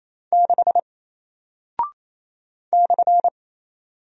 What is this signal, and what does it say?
Morse code, heard as a keyed tone, about 27 wpm, 700 Hz. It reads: 6 /
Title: Single Character (Letter:Number:Punctuation:Prosign) 28wpm